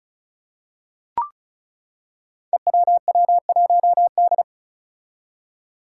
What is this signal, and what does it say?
Morse code, heard as a keyed tone, about 36 wpm, 700 Hz. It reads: EWW1D
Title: Five Challenging Letter-Number Combinations 35wpm